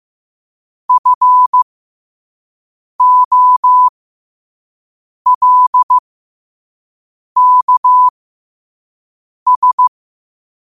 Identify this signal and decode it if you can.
Morse code, heard as a keyed tone, about 15 wpm, spaced as for 5 wpm, 1000 Hz. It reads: FOLKS
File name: qotdfiles / 5wpm